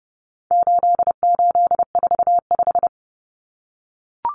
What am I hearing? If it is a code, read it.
Morse code, heard as a keyed tone, about 30 wpm, 700 Hz. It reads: 8845
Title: Numbers - 4-Digit 30wpm